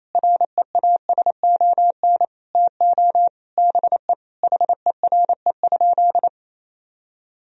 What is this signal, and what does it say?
Morse code, heard as a keyed tone, about 27 wpm, 700 Hz. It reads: REASON TO BE HERE?
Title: Sentences from Top 1000 Words - Review Part 4 - 28wpm